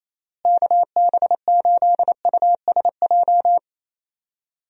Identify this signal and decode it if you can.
Morse code, heard as a keyed tone, about 28 wpm, 700 Hz. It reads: KB8USJ
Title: USA Call Signs - 6 Characters - Repeated 3x - 28wpm